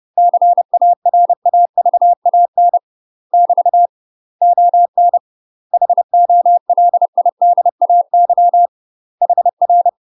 Morse code, heard as a keyed tone, about 30 wpm, 700 Hz. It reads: CARAVAN <BT> ON HOLIDAY HR
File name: UK MASTER QSOs TRAINING/30WPM